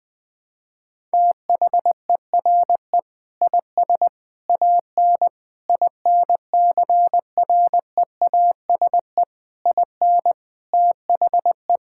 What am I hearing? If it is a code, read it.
Morse code, heard as a keyed tone, about 20 wpm, 700 Hz. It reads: THERE IS AN INCREASE IN THE
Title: Sentences with 5-Word Phrases - 20wpm